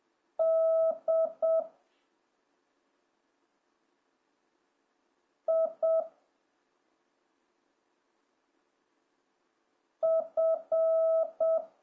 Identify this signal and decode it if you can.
Morse code, heard as a keyed tone, about 7 wpm, 650 Hz. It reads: D I F